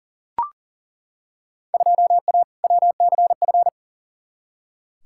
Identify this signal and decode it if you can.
Morse code, heard as a keyed tone, about 37 wpm, 700 Hz. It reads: 2A WCF
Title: Field Day Exchanges - 40wpm